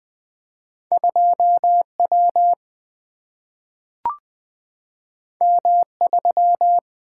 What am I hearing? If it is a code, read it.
Morse code, heard as a keyed tone, about 20 wpm, 700 Hz. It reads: 2 W M 3
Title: Two Letter-Number Combinations - 20wpm